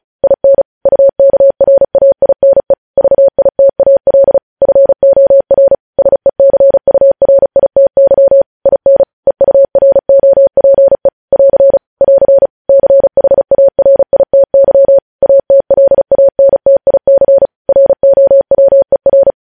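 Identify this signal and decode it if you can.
Morse code, heard as a keyed tone, about 36 wpm, 550 Hz. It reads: IN UKRAINE VITAL FOR SECURITY IN EUROPE <AR> <AR> CHARITY ATLANTIC ROWER
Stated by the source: Morse code practice reading news headlines at 35 words per minute.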